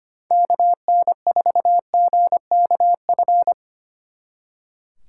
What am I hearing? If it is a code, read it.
Morse code, heard as a keyed tone, about 32 wpm, 700 Hz. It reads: KN4GKF